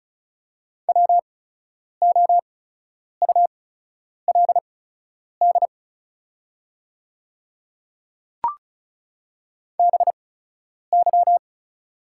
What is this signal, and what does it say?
Morse code, heard as a keyed tone, about 36 wpm, spaced as for 10 wpm, 700 Hz. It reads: WOULD BY
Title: 8x Character Spacing - Top 100 Words - 35wpm